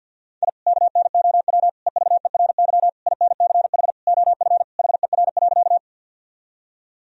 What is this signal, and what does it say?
Morse code, heard as a keyed tone, about 50 wpm, 700 Hz. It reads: I KNOW EVERY INCH OF HER.
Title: Sentences from Top 1000 Words - Review Part 3 - 50wpm